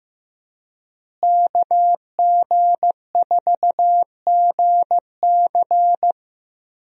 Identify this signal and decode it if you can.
Morse code, heard as a keyed tone, about 15 wpm, 700 Hz. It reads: KG4GC
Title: USA Call Signs - 5 Characters - Repeated 3x - 15wpm